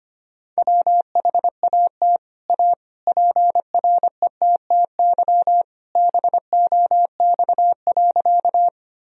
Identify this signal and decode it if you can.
Morse code, heard as a keyed tone, about 25 wpm, 700 Hz. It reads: WHAT A PRETTY BOX.
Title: Sentences from Top 1000 Words - Review Part 2 - 25wpm